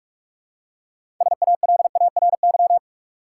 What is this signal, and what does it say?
Morse code, heard as a keyed tone, about 46 wpm, 700 Hz. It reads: SALARY
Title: Six Letter Words - 45wpm